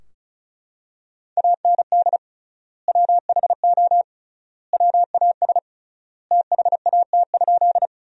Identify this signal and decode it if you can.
Morse code, heard as a keyed tone, about 34 wpm, 700 Hz. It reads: AND WHO WAS THAT?
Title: 3x Word-Spacing - Sentences from Top 100 Words - 35wpm